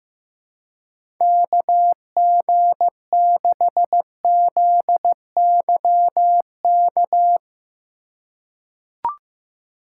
Morse code, heard as a keyed tone, about 15 wpm, 700 Hz. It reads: KG6ZYK
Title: USA Call Signs - 6 Characters - 3x Character Spacing - 15wpm